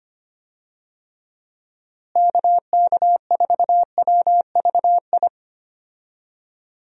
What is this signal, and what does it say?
Morse code, heard as a keyed tone, about 25 wpm, 700 Hz. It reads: KK4WVI
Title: USA Call Signs - 6 Characters - Repeated 3x - 25wpm